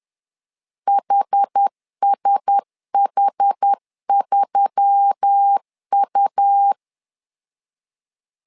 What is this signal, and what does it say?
Morse code, heard as a keyed tone, about 10 wpm, 800 Hz. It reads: HSH3U